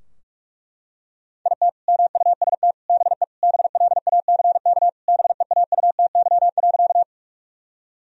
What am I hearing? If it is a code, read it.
Morse code, heard as a keyed tone, about 41 wpm, 700 Hz. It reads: IT MUST BE BLACK BEAUTY.
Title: Sentences from Top 1000 Words - Review Part 2 - 45wpm